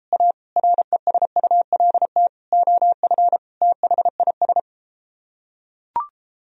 Morse code, heard as a keyed tone, about 34 wpm, 700 Hz. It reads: A RESULT OF THIS